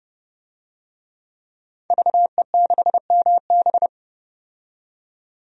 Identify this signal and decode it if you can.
Morse code, heard as a keyed tone, about 30 wpm, 700 Hz. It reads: VE6MB
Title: DXCC-100 Call Signs - 5 Characters - 30wpm